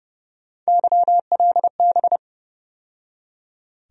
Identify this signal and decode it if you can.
Morse code, heard as a keyed tone, about 32 wpm, 700 Hz. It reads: YLB
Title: USA Call Sign Postfixes - 3 Letters 30wpm